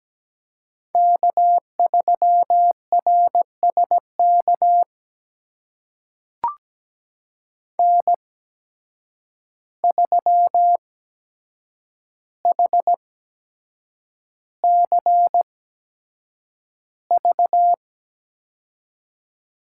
Morse code, heard as a keyed tone, about 17 wpm, spaced as for 7 wpm, 700 Hz. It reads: K3RSK N3HCV
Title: USA Call Signs - 5 Characters - 8x Character Spacing - 17wpm